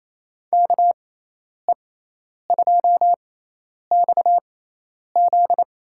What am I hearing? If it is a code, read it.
Morse code, heard as a keyed tone, about 28 wpm, spaced as for 10 wpm, 700 Hz. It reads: KE2XZ
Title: USA Call Signs - 6x Character Spacing - 28wpm